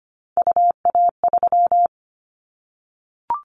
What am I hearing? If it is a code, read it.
Morse code, heard as a keyed tone, about 25 wpm, 700 Hz. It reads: UA3
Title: Callsign Prefixes for Top 50 DXCC - 25wpm